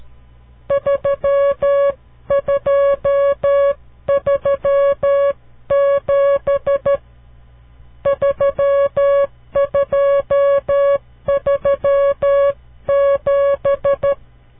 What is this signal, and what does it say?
Morse code, heard as a keyed tone, about 12 wpm, 560 Hz. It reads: 3237 3237
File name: CW Learning Programs / farnsworth / unit 5